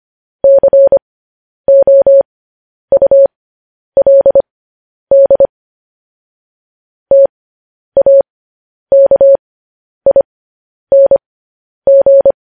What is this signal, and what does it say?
Morse code, heard as a keyed tone, about 25 wpm, spaced as for 10 wpm, 550 Hz. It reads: COULD TAKING
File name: podcast / News headlines-1x-25x10